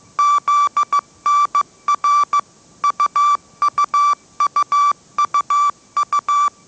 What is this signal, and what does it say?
Morse code, heard as a keyed tone, about 15 wpm, 1160 Hz. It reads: ZNR UUUUU
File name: mrx / mrx065 16wpm